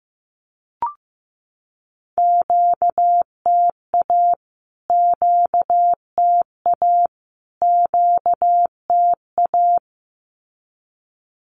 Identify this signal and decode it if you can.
Morse code, heard as a keyed tone, about 15 wpm, 700 Hz. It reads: QTA QTA QTA
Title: Q-Codes Repeated 3x 15wpm